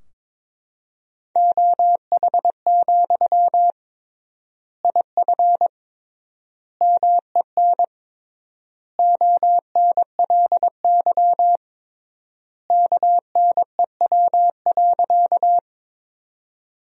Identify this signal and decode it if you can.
Morse code, heard as a keyed tone, about 22 wpm, 700 Hz. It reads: OH, IF MEN ONLY KNEW.
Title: 3x Word-Spacing - Sentences from Top 500 Words - 22wpm